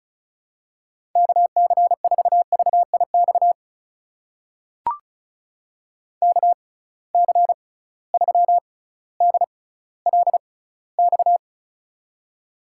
Morse code, heard as a keyed tone, about 36 wpm, spaced as for 18 wpm, 700 Hz. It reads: KC4VIX KC3DLX
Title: USA Call Signs - 6 Characters - 6x Character Spacing - 35wpm